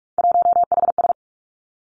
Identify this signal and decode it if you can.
Morse code, heard as a keyed tone, about 46 wpm, 700 Hz. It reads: 1HS
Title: Three Challenging Letter-Number Combinations 45wpm